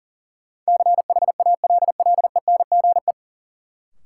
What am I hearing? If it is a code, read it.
Morse code, heard as a keyed tone, about 40 wpm, 700 Hz. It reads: CHALLENGE
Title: Sentences from Top 1000 Words - Review Part 7 - New Word Drill - 40wpm